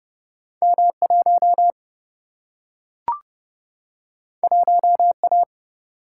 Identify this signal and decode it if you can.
Morse code, heard as a keyed tone, about 29 wpm, 700 Hz. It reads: M 1 1 A
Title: Single-Letter Single-Number 30wpm